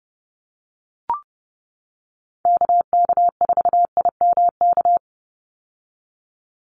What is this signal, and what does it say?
Morse code, heard as a keyed tone, about 30 wpm, 700 Hz. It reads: KK4IMK
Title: USA Call Signs 30wpm